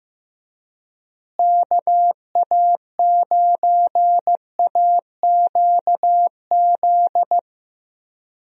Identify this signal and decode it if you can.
Morse code, heard as a keyed tone, about 15 wpm, 700 Hz. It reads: KA9AQZ
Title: USA Call Signs - 6 Characters - Repeated 3x - 15wpm